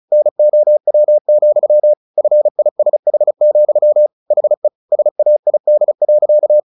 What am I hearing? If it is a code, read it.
Morse code, heard as a keyed tone, about 34 wpm, 600 Hz. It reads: NOW, FISH, HE SAID.
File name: the old man and the sea / the old man and the sea 1 35wpm